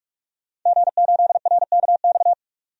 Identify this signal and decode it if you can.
Morse code, heard as a keyed tone, about 47 wpm, 700 Hz. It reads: G8RKX
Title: DXCC-100 Call Signs - 5 Characters - 45wpm